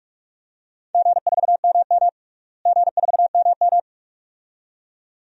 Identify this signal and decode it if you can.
Morse code, heard as a keyed tone, about 46 wpm, 700 Hz. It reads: G4MM G4MM
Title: DXCC-100 Call Signs - 4 Characters - Repeated 2x - 45wpm